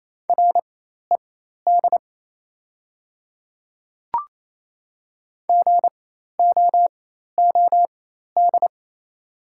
Morse code, heard as a keyed tone, about 28 wpm, spaced as for 13 wpm, 690 Hz. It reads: RED GOOD
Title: 4x Character Spacing - Top 500 Words - 28wpm